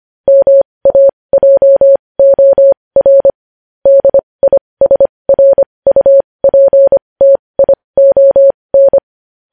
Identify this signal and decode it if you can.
Morse code, heard as a keyed tone, about 25 wpm, spaced as for 20 wpm, 550 Hz. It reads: MAJOR DISRUPTION
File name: podcast / News headlines-1x-25x20